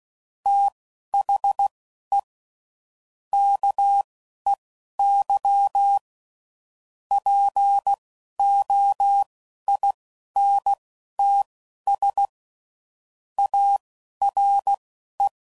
Morse code, heard as a keyed tone, about 16 wpm, spaced as for 11 wpm, 800 Hz. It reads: THE KEY POINTS ARE
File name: CW Code Groups @10WPM / Technical Subjects / Reactance